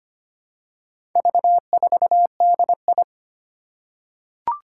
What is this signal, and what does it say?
Morse code, heard as a keyed tone, about 25 wpm, 700 Hz. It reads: V4DI
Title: Four Challenging Letter-Number Combinations 25wpm